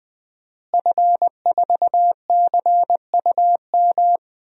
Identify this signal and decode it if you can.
Morse code, heard as a keyed tone, about 20 wpm, 700 Hz. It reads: F4CUM
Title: DXCC-100 Call Signs - 5 Characters - Repeated 2x - 20wpm